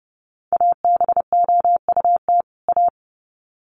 Morse code, heard as a keyed tone, about 30 wpm, 700 Hz. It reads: ABOUT A
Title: Sets of 2 Words 30wpm